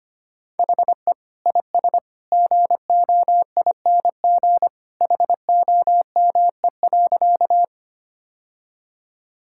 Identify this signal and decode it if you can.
Morse code, heard as a keyed tone, about 25 wpm, 700 Hz. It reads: HE IS GOING HOME.
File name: practice / 25wpm / Sentences with 2-Word Phrases - 25wpm